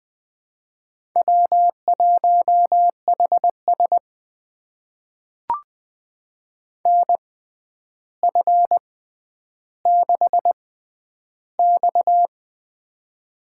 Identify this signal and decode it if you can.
Morse code, heard as a keyed tone, about 20 wpm, spaced as for 10 wpm, 700 Hz. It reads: W1HS NF6X
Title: USA Call Signs - 4 Characters - 6x Character Spacing - 20wpm